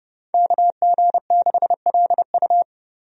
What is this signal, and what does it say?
Morse code, heard as a keyed tone, about 31 wpm, 700 Hz. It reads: KG6LU
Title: USA Call Signs - 5 Characters - 30wpm